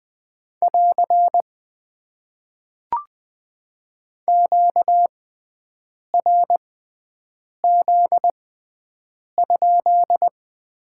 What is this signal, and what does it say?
Morse code, heard as a keyed tone, about 20 wpm, spaced as for 7 wpm, 700 Hz. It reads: <AR> QRZ?
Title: Top 90 QSO Elements - 6x Character Spacing - 20wpm